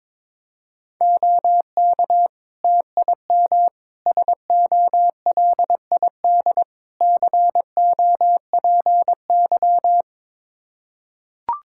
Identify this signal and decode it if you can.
Morse code, heard as a keyed tone, about 22 wpm, 700 Hz. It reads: OK TIM SOLID COPY
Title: Practice QSO - Exchange 3 - More Info - 22wpm